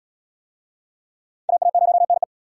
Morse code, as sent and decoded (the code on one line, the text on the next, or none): .- .- ----- -. .
AA0NE